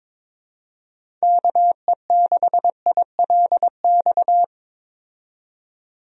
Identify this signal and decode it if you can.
Morse code, heard as a keyed tone, about 22 wpm, 700 Hz. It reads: KE6ILX
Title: USA Call Signs - 6 Characters - 22wpm